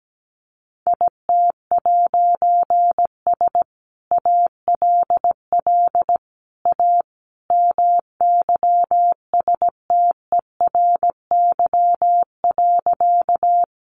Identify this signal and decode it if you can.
Morse code, heard as a keyed tone, about 17 wpm, 700 Hz. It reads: IT'S ALL A MYSTERY.
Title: A Sound Of Thunder 17wpm